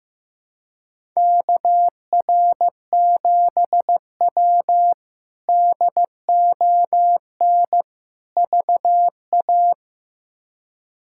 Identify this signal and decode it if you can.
Morse code, heard as a keyed tone, about 15 wpm, 700 Hz. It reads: KR7W DON VA
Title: K1USN SST QSO Practice - 15wpm